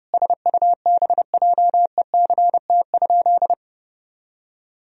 Morse code, heard as a keyed tone, about 30 wpm, 700 Hz. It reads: SUBJECT?
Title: Sentences from Top 1000 Words - Review Part 1 - 30wpm